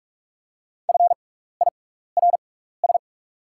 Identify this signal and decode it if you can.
Morse code, heard as a keyed tone, about 44 wpm, spaced as for 15 wpm, 700 Hz. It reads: FIRS
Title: 6x Character Spacing - Top 100 Words - 45wpm